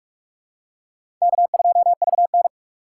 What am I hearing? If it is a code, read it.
Morse code, heard as a keyed tone, about 43 wpm, 700 Hz. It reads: K2VN
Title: USA Call Signs - 4 Characters - 6x Character Spacing - 45wpm